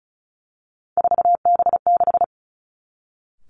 Morse code, heard as a keyed tone, about 36 wpm, 700 Hz. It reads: 4B6
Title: Three Challenging Letter-Number Combinations 35wpm